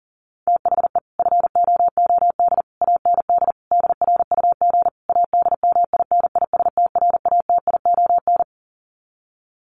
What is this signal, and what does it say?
Morse code, heard as a keyed tone, about 40 wpm, 700 Hz. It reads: THE FOOD AND DRUG ADMINISTRATION
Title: Sets of 5 Words 40wpm